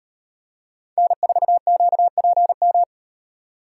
Morse code, heard as a keyed tone, about 38 wpm, 700 Hz. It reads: N4QPM